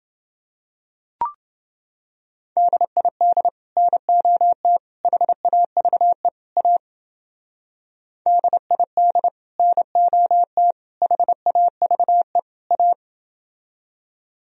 Morse code, heard as a keyed tone, about 29 wpm, 700 Hz. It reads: DID NOT HAVE A DID NOT HAVE A